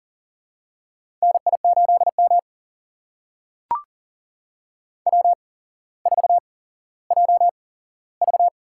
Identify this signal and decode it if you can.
Morse code, heard as a keyed tone, about 40 wpm, spaced as for 17 wpm, 700 Hz. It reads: NI8M W4JV
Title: USA Call Signs - 4 Characters - 8x Character Spacing - 40wpm